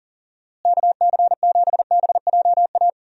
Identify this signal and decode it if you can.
Morse code, heard as a keyed tone, about 40 wpm, 700 Hz. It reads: KC7BJA
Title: USA Call Signs - 6 Characters - Repeated 3x - 40wpm